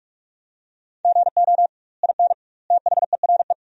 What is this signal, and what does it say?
Morse code, heard as a keyed tone, about 41 wpm, 700 Hz. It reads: GO IN THERE
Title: 2x Word-Spacing - Sentences from Top 500 Words - 45wpm